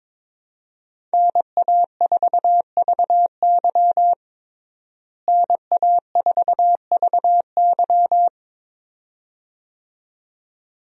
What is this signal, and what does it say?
Morse code, heard as a keyed tone, about 22 wpm, 700 Hz. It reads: NA4VY NA4VY
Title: USA Call Signs - 5 Characters - Repeated 2x - 22wpm